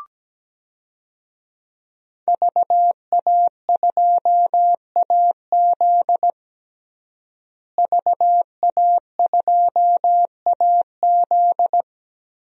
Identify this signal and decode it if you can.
Morse code, heard as a keyed tone, about 17 wpm, 700 Hz. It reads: VA2AZ VA2AZ
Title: DXCC-100 Call Signs - 5 Characters - Repeated 3x - 17wpm